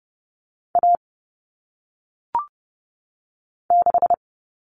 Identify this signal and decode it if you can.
Morse code, heard as a keyed tone, about 30 wpm, 700 Hz. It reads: A 6
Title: All Cut Numbers - 1-Digit 30wpm